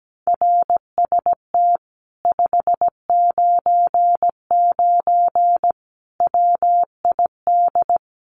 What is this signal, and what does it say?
Morse code, heard as a keyed tone, about 17 wpm, 700 Hz. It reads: RST 599 WID
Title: Common QSO Phrases Encore 17wpm